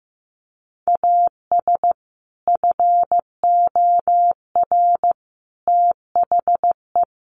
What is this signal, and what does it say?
Morse code, heard as a keyed tone, about 15 wpm, 700 Hz. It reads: AS FOR THE
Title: Sentences from Top 1100 Words 15wpm